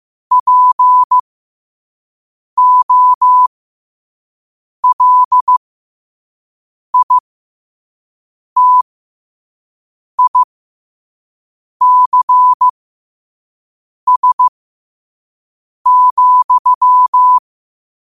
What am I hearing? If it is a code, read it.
Morse code, heard as a keyed tone, about 15 wpm, spaced as for 5 wpm, 1000 Hz. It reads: POLITICS,
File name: qotdfiles / 5wpm